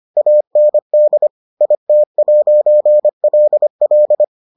Morse code, heard as a keyed tone, about 25 wpm, 600 Hz. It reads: AND IT'LL